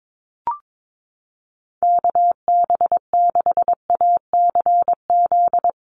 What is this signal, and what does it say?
Morse code, heard as a keyed tone, about 22 wpm, 700 Hz. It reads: KB6ACZ
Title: USA Call Signs 22wpm